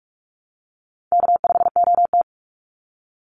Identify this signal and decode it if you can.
Morse code, heard as a keyed tone, about 44 wpm, 700 Hz. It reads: K5OT
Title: NA SOTA Chaser Callsigns 45wpm